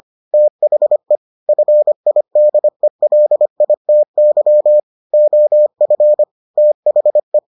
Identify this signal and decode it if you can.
Morse code, heard as a keyed tone, about 25 wpm, 600 Hz. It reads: THE FIDELITY OF THE